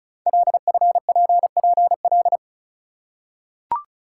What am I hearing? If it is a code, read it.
Morse code, heard as a keyed tone, about 36 wpm, 690 Hz. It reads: LFPPL
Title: Code Groups - 5 Letters - 35wpm